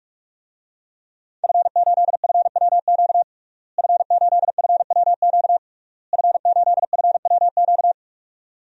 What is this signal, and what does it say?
Morse code, heard as a keyed tone, about 46 wpm, 700 Hz. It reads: F8FWQ F8FWQ F8FWQ
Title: DXCC-100 Call Signs - 5 Characters - Repeated 3x - 45wpm